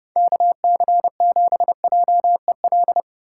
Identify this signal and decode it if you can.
Morse code, heard as a keyed tone, about 30 wpm, 700 Hz. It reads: KC7JEL
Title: USA Call Signs - 6 Characters - 2x Character Spacing - 30wpm